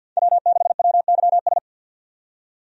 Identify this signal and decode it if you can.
Morse code, heard as a keyed tone, about 50 wpm, 700 Hz. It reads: W6WYS
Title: USA Call Signs - 5 Characters - 50wpm